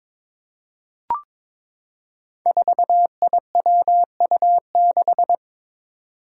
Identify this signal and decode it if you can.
Morse code, heard as a keyed tone, about 22 wpm, 700 Hz. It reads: 4IWU6
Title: Five Challenging Letter-Number Combinations 22wpm